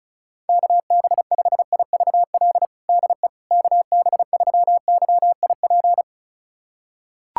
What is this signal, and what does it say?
Morse code, heard as a keyed tone, about 36 wpm, 700 Hz. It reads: KB5IVL DE KB3YIP
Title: USA Callsign DE Callsign - 6 Characters - 35wpm